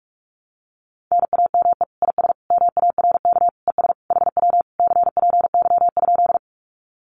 Morse code, heard as a keyed tone, about 44 wpm, 700 Hz. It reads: NAME IS MARK ES HW CPY?
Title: Common QSO Phrases Encore 45wpm